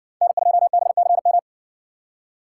Morse code, heard as a keyed tone, about 60 wpm, 700 Hz. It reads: N2BCM